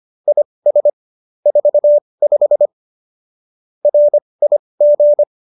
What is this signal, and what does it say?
Morse code, heard as a keyed tone, about 25 wpm, spaced as for 20 wpm, 590 Hz. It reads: IS 45 RIG